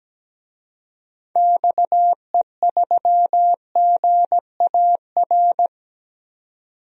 Morse code, heard as a keyed tone, about 17 wpm, 700 Hz. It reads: XE3GAR
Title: DXCC-100 Call Signs - 6 Characters - 17wpm